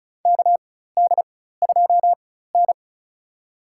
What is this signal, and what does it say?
Morse code, heard as a keyed tone, about 36 wpm, spaced as for 16 wpm, 700 Hz. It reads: KD2N
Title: USA Call Signs - 4 Characters - 4x Character Spacing - 35wpm